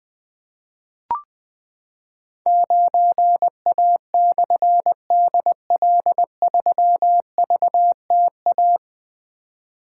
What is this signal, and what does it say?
Morse code, heard as a keyed tone, about 20 wpm, 700 Hz. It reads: 9A/DL3VTA
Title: Difficult Call Signs 20wpm